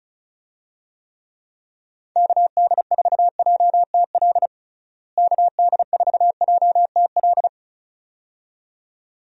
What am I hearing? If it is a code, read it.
Morse code, heard as a keyed tone, about 36 wpm, 700 Hz. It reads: KD4JTL KD4JTL
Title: USA Call Signs - 6 Characters - Repeated 2x - 35wpm